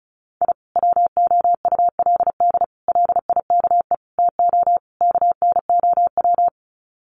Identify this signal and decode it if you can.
Morse code, heard as a keyed tone, about 34 wpm, 700 Hz. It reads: I WOULD LIKE TO KNOW
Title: Sets of 5 Words Encore 35wpm